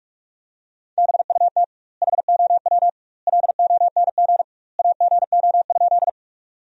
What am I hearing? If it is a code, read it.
Morse code, heard as a keyed tone, about 45 wpm, 700 Hz. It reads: BUT HOW LONG AGO?
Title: 2x Word-Spacing - Sentences from Top 500 Words - 45wpm